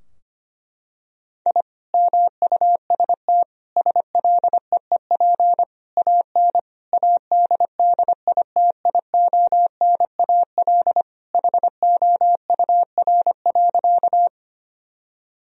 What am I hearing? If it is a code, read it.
Morse code, heard as a keyed tone, about 25 wpm, 700 Hz. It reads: I MUST SLEEP AN ADDITIONAL HOUR.
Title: Sentences from Top 1000 Words - Review Part 4 - 25wpm